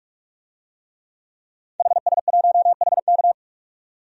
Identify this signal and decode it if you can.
Morse code, heard as a keyed tone, about 44 wpm, 690 Hz. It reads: HS1HK
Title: DXCC-100 Call Signs - 5 Characters - 45wpm